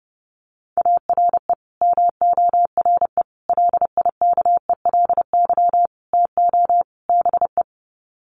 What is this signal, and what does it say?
Morse code, heard as a keyed tone, about 30 wpm, 700 Hz. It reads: ARE MORE LIKELY TO BE
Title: Sets of 5 Words 30wpm